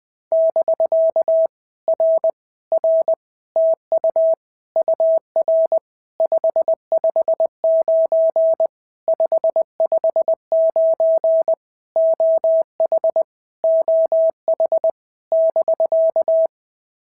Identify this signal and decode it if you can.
Morse code, heard as a keyed tone, about 20 wpm, 650 Hz. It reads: <BK> R R TU UR 559 559 OH OH <BK>
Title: POTA QSO Practice - 20wpm